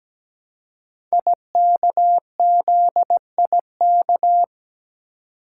Morse code, heard as a keyed tone, about 17 wpm, 700 Hz. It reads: IKZIK
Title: Code Groups - 5 Letters - 17wpm